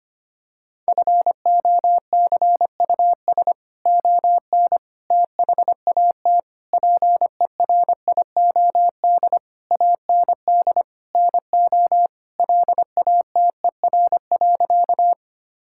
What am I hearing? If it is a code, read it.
Morse code, heard as a keyed tone, about 25 wpm, 700 Hz. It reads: FOCUS ON THAT PERIOD AND NO LATER.
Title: Sentences from Top 1000 Words - Review Part 5 - 25wpm